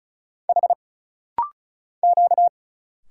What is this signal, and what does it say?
Morse code, heard as a keyed tone, about 35 wpm, 700 Hz. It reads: H Q
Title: Single Letters - Rapid-Fire 35wpm